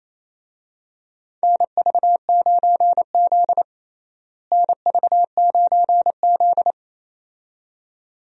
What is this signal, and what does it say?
Morse code, heard as a keyed tone, about 27 wpm, 700 Hz. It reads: NV9Z NV9Z
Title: USA Call Signs - 4 Characters - Repeated 2x - 28wpm